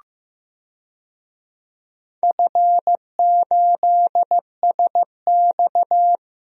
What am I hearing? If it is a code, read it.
Morse code, heard as a keyed tone, about 15 wpm, 700 Hz. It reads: F8SX
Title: DXCC-100 Call Signs - 4 Characters - 15wpm